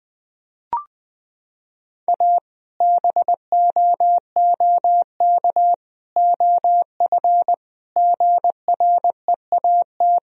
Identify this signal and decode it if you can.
Morse code, heard as a keyed tone, about 20 wpm, 700 Hz. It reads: A BOOK OF GREAT
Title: Sentences from Top 1000 Words - Review Part 2 - 20wpm